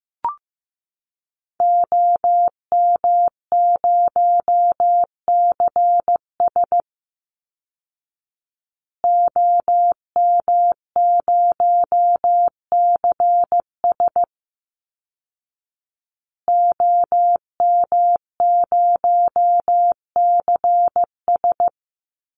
Morse code, heard as a keyed tone, about 15 wpm, 700 Hz. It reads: OM0CS OM0CS OM0CS
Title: Call Signs Repeated 2x - 15wpm